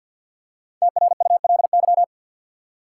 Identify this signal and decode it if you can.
Morse code, heard as a keyed tone, about 48 wpm, 700 Hz. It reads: TRULY
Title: Five Letter Words - 50wpm